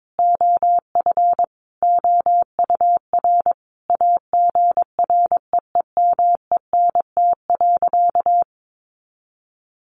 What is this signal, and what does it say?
Morse code, heard as a keyed tone, about 22 wpm, 700 Hz. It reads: OF OUR AGREEMENT.
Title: Sentences from Top 1500 Words 22wpm